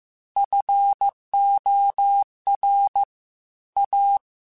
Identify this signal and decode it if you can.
Morse code, heard as a keyed tone, about 15 wpm, 800 Hz. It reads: FOR A
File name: Technical Subjects / Technical Subjects-mp3s / Sw Pwr Sup @13